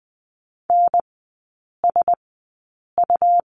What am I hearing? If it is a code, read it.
Morse code, heard as a keyed tone, about 20 wpm, spaced as for 8 wpm, 700 Hz. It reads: NSU
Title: 100 Lesson 17e - Letter u - Three Characters 20wpm